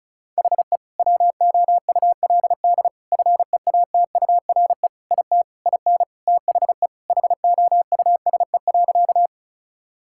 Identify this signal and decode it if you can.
Morse code, heard as a keyed tone, about 34 wpm, 700 Hz. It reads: HE WOULD FEATURE IT IN THE HOUSE.
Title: Sentences from Top 1000 Words - Review Part 5 - 35wpm